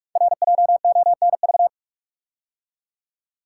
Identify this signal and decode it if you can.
Morse code, heard as a keyed tone, about 44 wpm, 700 Hz. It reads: RJONV
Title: Code Groups - 5 Letters - 45wpm